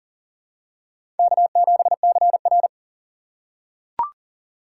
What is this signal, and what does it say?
Morse code, heard as a keyed tone, about 40 wpm, 700 Hz. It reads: K7CR
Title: USA Call Signs - 4 Characters - 3x Character Spacing - 40wpm